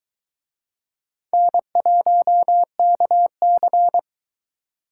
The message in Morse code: N1KC

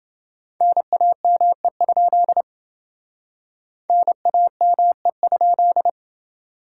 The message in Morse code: NAME? NAME?

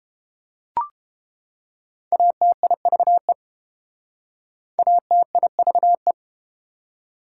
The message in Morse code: ATIVE ATIVE

